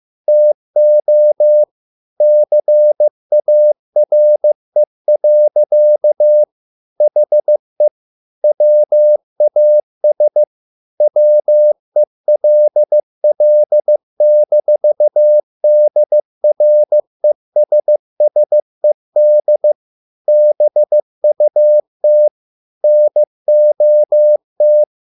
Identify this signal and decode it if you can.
Morse code, heard as a keyed tone, about 15 wpm, 600 Hz. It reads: TO CARE. HE WAS WELL-DRESSED BUT NOT